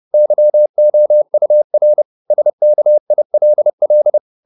Morse code, heard as a keyed tone, about 30 wpm, 600 Hz. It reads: YOUR SKILL